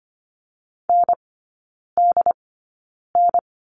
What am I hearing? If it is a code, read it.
Morse code, heard as a keyed tone, about 25 wpm, spaced as for 9 wpm, 690 Hz. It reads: NDN
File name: Lesson 15e - Number 5 - Three Characters 25wpm